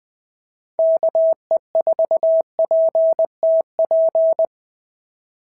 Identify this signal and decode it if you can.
Morse code, heard as a keyed tone, about 20 wpm, 690 Hz. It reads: KE4PTP
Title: POTA QSO Practice - 20wpm